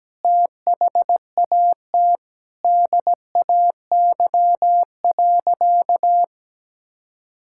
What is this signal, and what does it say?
Morse code, heard as a keyed tone, about 17 wpm, 700 Hz. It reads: THAT DAY.
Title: Sentences from Top 1000 Words - Review Part 5 - 17wpm